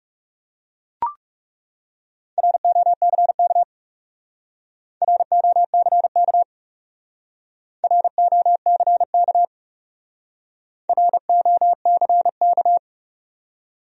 Morse code, heard as a keyed tone, about 36 wpm, 700 Hz. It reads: ROCK ROCK ROCK ROCK
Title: Speed-Racing Top 500 Words 30wpm